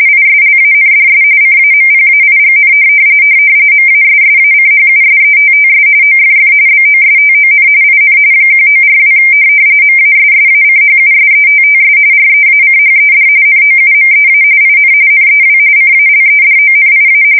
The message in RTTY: ISLANDS CAMPAIGN MASSACHUSETTS OFF BOSTON IN 1942 MASSACHUSETTS REACHED NOUMEA IN NEW CALEDONIA

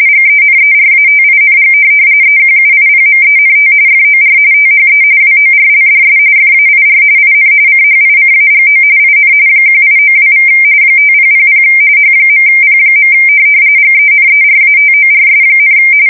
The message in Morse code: none

